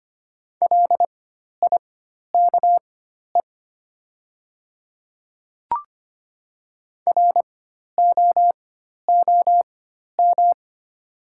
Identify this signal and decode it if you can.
Morse code, heard as a keyed tone, about 25 wpm, spaced as for 12 wpm, 690 Hz. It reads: LIKE ROOM
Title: 4x Character Spacing - Top 500 Words - 25wpm